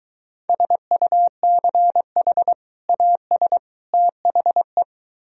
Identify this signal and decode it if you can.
Morse code, heard as a keyed tone, about 22 wpm, 700 Hz. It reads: SUCH AS THE